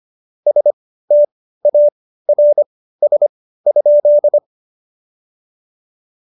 Morse code, heard as a keyed tone, about 25 wpm, spaced as for 15 wpm, 590 Hz. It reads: STARS?
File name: Jokes-25-char-15-eff